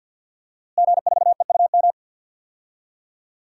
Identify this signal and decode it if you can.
Morse code, heard as a keyed tone, about 52 wpm, 700 Hz. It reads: G4EUM
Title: DXCC-100 Call Signs - 5 Characters - Repeated 3x - 50wpm